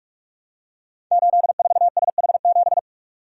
Morse code, heard as a keyed tone, about 44 wpm, 700 Hz. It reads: 84SH7
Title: Code Groups - 5 Letter:Numbers - 45wpm